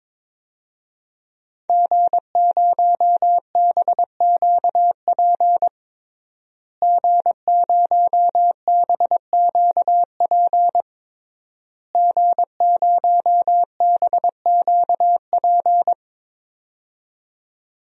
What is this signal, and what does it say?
Morse code, heard as a keyed tone, about 22 wpm, 700 Hz. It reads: G0BQP G0BQP G0BQP
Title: DXCC-100 Call Signs - 5 Characters - Repeated 3x - 22wpm